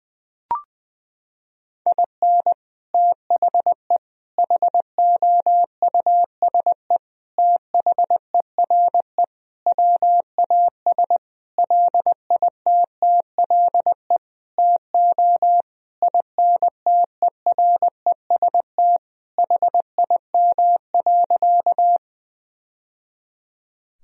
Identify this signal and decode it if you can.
Morse code, heard as a keyed tone, about 20 wpm, 700 Hz. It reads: IN THE HOUSE THERE WAS LITTLE TO INTEREST HIM.
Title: Sentences from Top 1000 Words - Review Part 2 - 20wpm